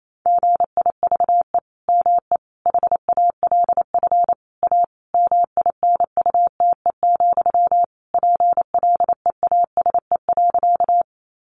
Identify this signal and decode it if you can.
Morse code, heard as a keyed tone, about 28 wpm, 700 Hz. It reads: GIVE ME HALF A MINUTE, PLEASE.
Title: Sentences from Top 1500 Words 28wpm